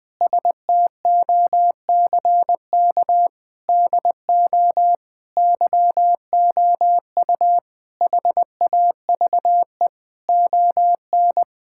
Morse code, heard as a keyed tone, about 20 wpm, 700 Hz. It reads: STOCK DO YOU HAVE ON
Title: Sentences from Top 1000 Words - Review Part 5 - 20wpm